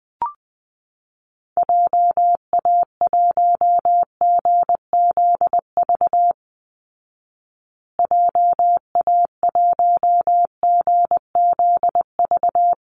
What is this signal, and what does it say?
Morse code, heard as a keyed tone, about 20 wpm, 700 Hz. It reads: JA1GZV JA1GZV
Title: Call Signs Repeated 2x - 20wpm